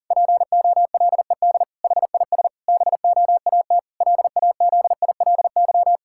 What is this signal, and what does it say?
Morse code, heard as a keyed tone, about 40 wpm, 700 Hz. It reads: POLED HIS BOAT LAZILY
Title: Pangram Challenge v2 - 40wpm